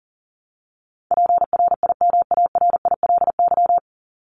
Morse code, heard as a keyed tone, about 40 wpm, 700 Hz. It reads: PRIMARILY
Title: LY Suffix - Top 100 Words with Base Words 40wpm